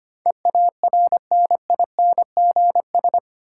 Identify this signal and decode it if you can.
Morse code, heard as a keyed tone, about 25 wpm, 700 Hz. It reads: EARNINGS
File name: Eight Letter Words - 25wpm